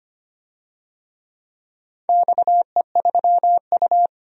Morse code, heard as a keyed tone, about 25 wpm, 700 Hz. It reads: XE3U
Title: DXCC-100 Call Signs - 4 Characters - 25wpm